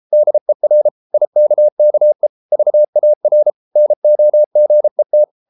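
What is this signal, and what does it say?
Morse code, heard as a keyed tone, about 33 wpm, 600 Hz. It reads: DER IKKE VAR NOGET
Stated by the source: Morse code at 33 wpm, QRQ.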